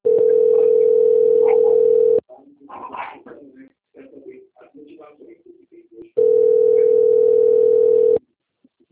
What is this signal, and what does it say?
Ringback tone (North America), 440 + 480 Hz, with speech.
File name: out- - - - - .49509